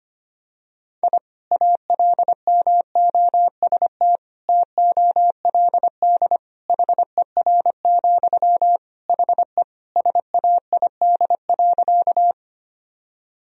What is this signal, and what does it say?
Morse code, heard as a keyed tone, about 25 wpm, 700 Hz. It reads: I ALMOST TOLD HER, HE SAID.
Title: Sentences from Top 1000 Words - Review Part 4 - 25wpm